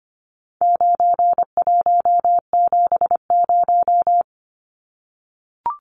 Morse code, heard as a keyed tone, about 25 wpm, 700 Hz. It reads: 9170
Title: Numbers - 4-Digit 25wpm